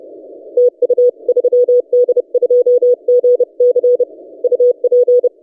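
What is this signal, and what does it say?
Morse code, heard as a keyed tone, about 30 wpm, 490 Hz. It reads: TU 3D2GC UP
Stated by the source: DX station 3D2GC on 20 metres CW, recorded by N6ML.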